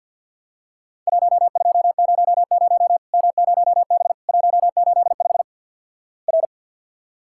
Morse code, heard as a keyed tone, about 48 wpm, 700 Hz. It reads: 1200 M0B 185 R
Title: Worked All Europe DX Contest - QTC Exchange - 50wpm